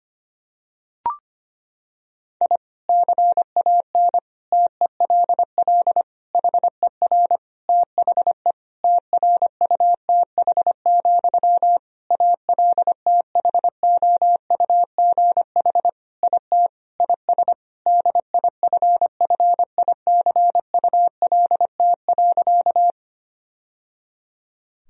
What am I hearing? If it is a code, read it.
Morse code, heard as a keyed tone, about 25 wpm, 700 Hz. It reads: I CAN TELL HER THE TRUTH, ALTHOUGH IT IS DIFFICULT.
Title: Sentences from Top 1000 Words - Review Part 3 - 25wpm